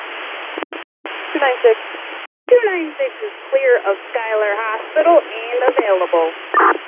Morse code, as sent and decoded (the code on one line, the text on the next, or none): none